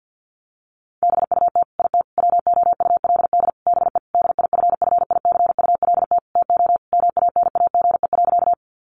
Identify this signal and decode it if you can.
Morse code, heard as a keyed tone, about 50 wpm, 700 Hz. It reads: BUT IT WOULD BE DIFFICULT TO MANAGE.
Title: Sentences from Top 1400 Words 50wpm